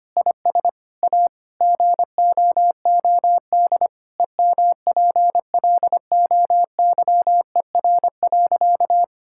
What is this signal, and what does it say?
Morse code, heard as a keyed tone, about 25 wpm, 700 Hz. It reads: IS A GOOD EMPLOYER.
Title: Sentences from Top 1600 Words 25wpm